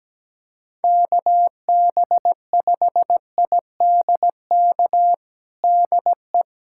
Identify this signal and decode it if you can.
Morse code, heard as a keyed tone, about 17 wpm, 700 Hz. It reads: KB5IDK DE
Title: USA Callsign DE Callsign - 6 Characters - 17wpm